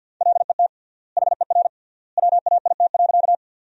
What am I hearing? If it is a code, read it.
Morse code, heard as a keyed tone, about 47 wpm, 700 Hz. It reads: LET HER WAIT.
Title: 3x Word-Spacing - Sentences from Top 500 Words - 50wpm